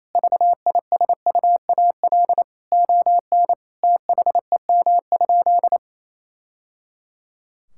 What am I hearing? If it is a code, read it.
Morse code, heard as a keyed tone, about 27 wpm, 700 Hz. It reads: VISUAL ON THEM?
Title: Sentences from Top 1000 Words - Review Part 6 - 28wpm